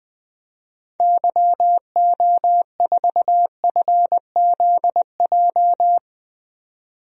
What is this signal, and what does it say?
Morse code, heard as a keyed tone, about 20 wpm, 700 Hz. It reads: YO4FZJ
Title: DXCC-100 Call Signs - 6 Characters - 20wpm